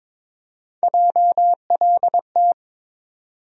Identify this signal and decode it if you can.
Morse code, heard as a keyed tone, about 22 wpm, 700 Hz. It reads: JLT
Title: USA Call Sign Postfixes - 3 Letters 22wpm